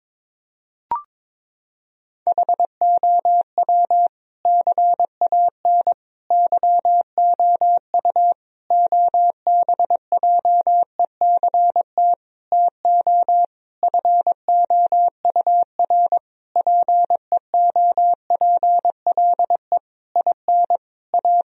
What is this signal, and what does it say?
Morse code, heard as a keyed tone, about 22 wpm, 700 Hz. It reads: HOW CAN YOU OBJECT TO FOUR PEOPLE IN A
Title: Sentences from Top 1000 Words - Review Part 3 - 22wpm